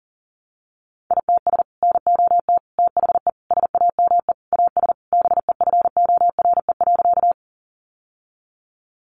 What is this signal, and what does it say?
Morse code, heard as a keyed tone, about 40 wpm, 700 Hz. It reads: ITS NOT THE SAME AS BEFORE.
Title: Sentences from Top 600 Words 40wpm